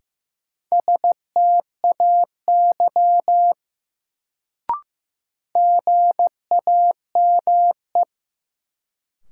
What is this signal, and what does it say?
Morse code, heard as a keyed tone, about 15 wpm, 700 Hz. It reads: STAY GAME
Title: Sentences from Top 500 Words - New Word Drill - 15wpm